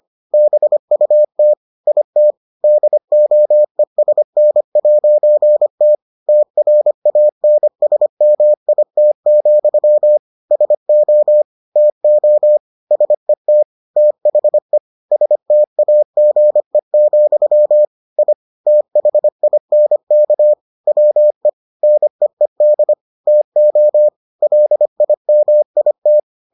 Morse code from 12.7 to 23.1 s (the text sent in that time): SET THE STAGE, I THINK WE NEED